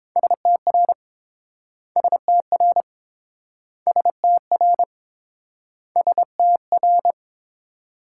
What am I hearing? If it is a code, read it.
Morse code, heard as a keyed tone, about 30 wpm, 700 Hz. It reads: STR STR STR STR